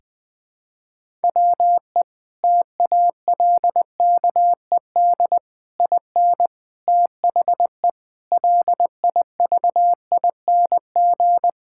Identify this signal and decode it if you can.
Morse code, heard as a keyed tone, about 20 wpm, 700 Hz. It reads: WE TALKED IN THE LIVING
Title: Sentences with 4-Word Phrases - 20wpm